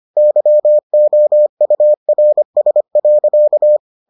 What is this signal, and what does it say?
Morse code, heard as a keyed tone, about 25 wpm, 600 Hz. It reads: YOURS.